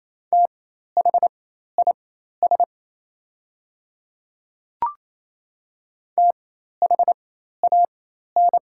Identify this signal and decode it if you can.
Morse code, heard as a keyed tone, about 28 wpm, spaced as for 13 wpm, 700 Hz. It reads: THIS THAN
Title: 4x Character Spacing - Top 100 Words - 28wpm